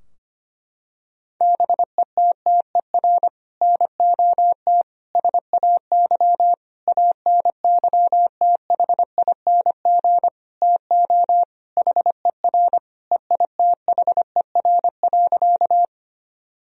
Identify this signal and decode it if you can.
Morse code, heard as a keyed tone, about 25 wpm, 700 Hz. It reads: BETTER NOT SAY ANYTHING TO HER EITHER.
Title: Sentences from Top 1000 Words - Review Part 4 - 25wpm